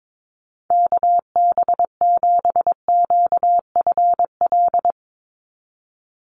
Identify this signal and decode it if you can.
Morse code, heard as a keyed tone, about 22 wpm, 700 Hz. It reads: KB7QFL
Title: USA Call Signs 22wpm